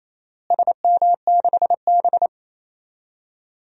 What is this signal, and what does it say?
Morse code, heard as a keyed tone, about 28 wpm, 700 Hz. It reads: SM6B
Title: DXCC-100 Call Signs - 4 Characters - Repeated 2x - 28wpm